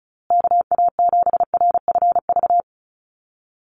Morse code, heard as a keyed tone, about 34 wpm, 700 Hz. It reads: KA7RFV
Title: USA Call Signs Repeated 2x 35wpm